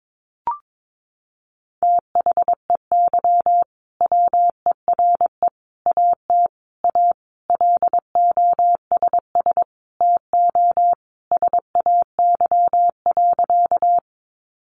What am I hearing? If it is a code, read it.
Morse code, heard as a keyed tone, about 22 wpm, 700 Hz. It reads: THEY WERE AT A LOSS TO SAY.
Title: Sentences from Top 1400 Words 22wpm